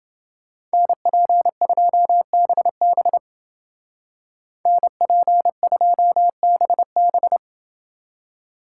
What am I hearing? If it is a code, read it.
Morse code, heard as a keyed tone, about 29 wpm, 700 Hz. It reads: NP2BB NP2BB